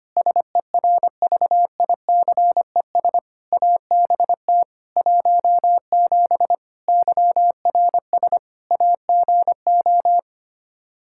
Morse code, heard as a keyed tone, about 25 wpm, 700 Hz. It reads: SERVICES ABT 17 YRS AGO
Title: K1USN SST QSO Practice 25wpm